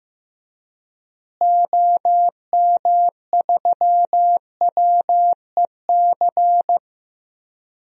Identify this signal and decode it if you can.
Morse code, heard as a keyed tone, about 16 wpm, 700 Hz. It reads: OM3WEC
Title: DXCC-100 Call Signs - 6 Characters - Repeated 2x - 15wpm